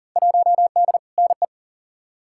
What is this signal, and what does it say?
Morse code, heard as a keyed tone, about 40 wpm, 700 Hz. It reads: 1D NE
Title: Field Day Exchanges - 40wpm